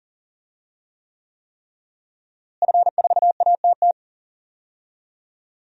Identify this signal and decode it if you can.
Morse code, heard as a keyed tone, about 40 wpm, 700 Hz. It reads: F4ATT
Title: DXCC-100 Call Signs - 5 Characters - Repeated 3x - 40wpm